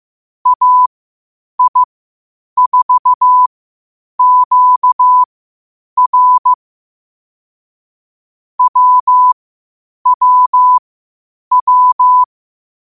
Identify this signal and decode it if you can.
Morse code, heard as a keyed tone, about 15 wpm, spaced as for 8 wpm, 1000 Hz. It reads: AI4QR WWW
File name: qotdfiles / 8wpm